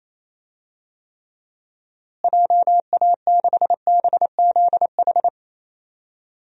Morse code, heard as a keyed tone, about 28 wpm, 700 Hz. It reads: JA6BZH
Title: DXCC-100 Call Signs - 6 Characters - Repeated 2x - 28wpm